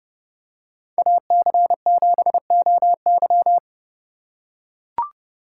Morse code, heard as a keyed tone, about 30 wpm, 700 Hz. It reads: AC7OY